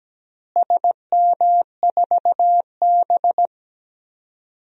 Morse code, heard as a keyed tone, about 17 wpm, 700 Hz. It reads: SM4B
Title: DXCC-100 Call Signs - 4 Characters - Repeated 3x - 17wpm